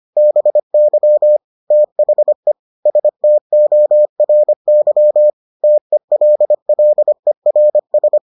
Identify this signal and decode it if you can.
Morse code, heard as a keyed tone, about 25 wpm, 600 Hz. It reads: BY THE STORY TELLERS